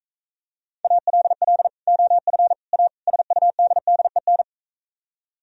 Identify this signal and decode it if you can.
Morse code, heard as a keyed tone, about 44 wpm, 700 Hz. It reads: ALL OF A SUDDEN